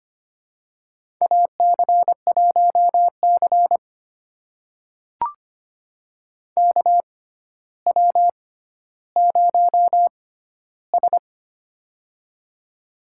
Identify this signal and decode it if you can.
Morse code, heard as a keyed tone, about 25 wpm, spaced as for 13 wpm, 700 Hz. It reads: AC1C KW0S
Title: USA Call Signs - 4 Characters - 6x Character Spacing - 25wpm